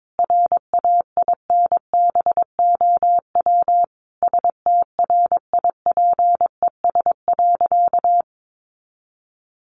Morse code, heard as a keyed tone, about 22 wpm, 700 Hz. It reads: RAINBOW STRIPES.
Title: Additional Easy Phrases 22wpm